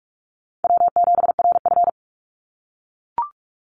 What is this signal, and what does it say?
Morse code, heard as a keyed tone, about 44 wpm, 700 Hz. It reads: W7RF
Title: Call Signs - 45wpm